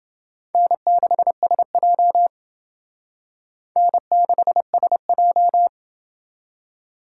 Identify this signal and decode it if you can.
Morse code, heard as a keyed tone, about 27 wpm, 700 Hz. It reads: N6SJ N6SJ